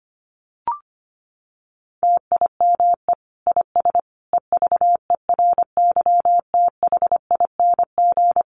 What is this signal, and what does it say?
Morse code, heard as a keyed tone, about 25 wpm, 700 Hz. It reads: TIME IS EVERYTHING
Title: Sentences from Top 700 Words 25wpm